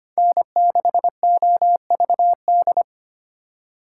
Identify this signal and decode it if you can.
Morse code, heard as a keyed tone, about 32 wpm, 700 Hz. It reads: N6OVD